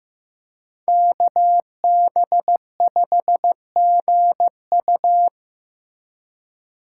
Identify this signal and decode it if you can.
Morse code, heard as a keyed tone, about 15 wpm, 700 Hz. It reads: KB5GU
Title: USA Call Signs - 2x Character Spacing - 15wpm